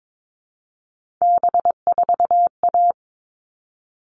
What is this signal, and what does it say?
Morse code, heard as a keyed tone, about 22 wpm, 700 Hz. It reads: B4A
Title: All Cut Numbers - 3-Digit 22wpm